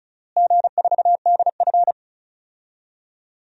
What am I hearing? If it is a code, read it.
Morse code, heard as a keyed tone, about 34 wpm, 700 Hz. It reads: G4DF
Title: DXCC-100 Call Signs - 4 Characters - Repeated 2x - 35wpm